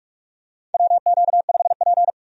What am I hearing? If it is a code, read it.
Morse code, heard as a keyed tone, about 45 wpm, 700 Hz. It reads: WQ5P
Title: USA Call Signs - 4 Characters - 45wpm